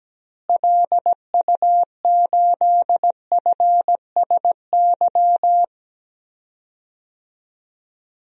Morse code, heard as a keyed tone, about 17 wpm, 700 Hz. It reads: LU8FSY
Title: DXCC-100 Call Signs - 6 Characters - Repeated 2x - 17wpm